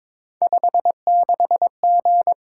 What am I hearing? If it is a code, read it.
Morse code, heard as a keyed tone, about 23 wpm, 700 Hz. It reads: 56G
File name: practice / 22wpm / Three Letter-Number Combinations - 22wpm